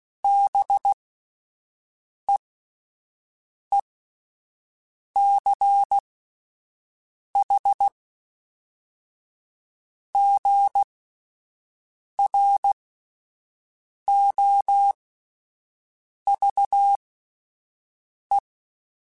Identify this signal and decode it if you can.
Morse code, heard as a keyed tone, about 16 wpm, spaced as for 5 wpm, 800 Hz. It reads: BEECHGROVE
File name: fromSMs / FromSMs mp3s / Cities@5